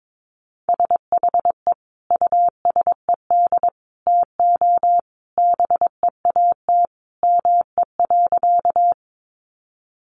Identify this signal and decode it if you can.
Morse code, heard as a keyed tone, about 22 wpm, 700 Hz. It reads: SHE USED TO BEAT ME.
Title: Sentences from Top 1300 Words 22wpm